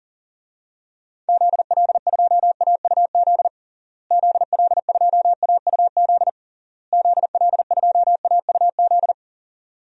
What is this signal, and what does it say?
Morse code, heard as a keyed tone, about 40 wpm, 700 Hz. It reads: ZL2AUZ ZL2AUZ ZL2AUZ
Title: DXCC-100 Call Signs - 6 Characters - Repeated 3x - 40wpm